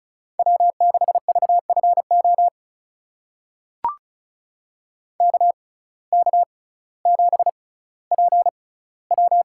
Morse code, heard as a keyed tone, about 34 wpm, spaced as for 18 wpm, 700 Hz. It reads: W6VFO KK7PW
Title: USA Call Signs - 5 Characters - 6x Character Spacing - 35wpm